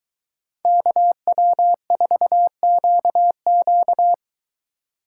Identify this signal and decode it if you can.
Morse code, heard as a keyed tone, about 23 wpm, 700 Hz. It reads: KW4QQ